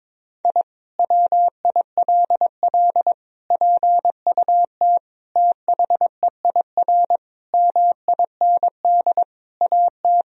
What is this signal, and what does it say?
Morse code, heard as a keyed tone, about 22 wpm, 700 Hz. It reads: I WILL PUT THEIR MIND AT
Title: Sentences from Top 1000 Words - Review Part 1 - 22wpm